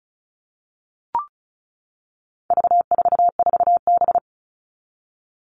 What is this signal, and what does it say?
Morse code, heard as a keyed tone, about 34 wpm, 700 Hz. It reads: V44B
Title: All Cut Numbers - 4-Digit 35wpm